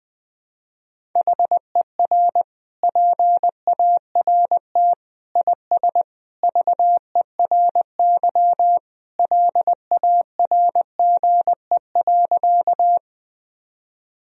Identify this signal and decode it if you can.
Morse code, heard as a keyed tone, about 20 wpm, 700 Hz. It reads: HER PART IS VERY LARGE.
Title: Sentences from Top 1000 Words - Review Part 1 - 20wpm